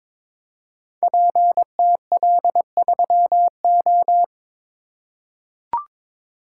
Morse code, heard as a keyed tone, about 23 wpm, 700 Hz. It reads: PTL3O
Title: Code Groups - 5 Letter:Numbers - 22wpm